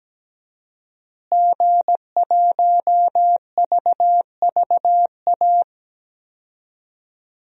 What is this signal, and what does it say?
Morse code, heard as a keyed tone, about 17 wpm, 700 Hz. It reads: G1VVA
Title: DXCC-100 Call Signs - 5 Characters - Repeated 2x - 17wpm